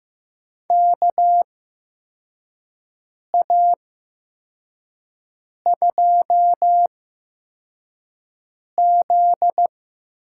Morse code, heard as a keyed tone, about 15 wpm, spaced as for 4 wpm, 700 Hz. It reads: KA2Z
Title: USA Call Signs - 6 Characters - 8x Character Spacing - 15wpm